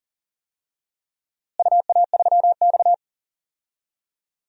Morse code, heard as a keyed tone, about 40 wpm, 700 Hz. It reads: UA3X
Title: DXCC-100 Call Signs - 4 Characters - Repeated 2x - 40wpm